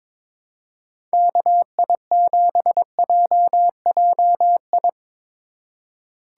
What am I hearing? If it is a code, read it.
Morse code, heard as a keyed tone, about 22 wpm, 700 Hz. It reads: KI7JJI
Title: USA Call Signs - 2x Character Spacing - 22wpm